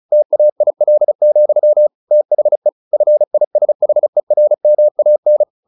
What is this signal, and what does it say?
Morse code, heard as a keyed tone, about 34 wpm, 600 Hz. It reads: TAIL, THE FISHERMAN
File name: the old man and the sea 2 35wpm